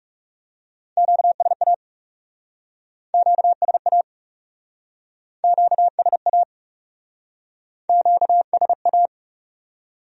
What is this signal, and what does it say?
Morse code, heard as a keyed tone, about 40 wpm, 700 Hz. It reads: QSA QSA QSA QSA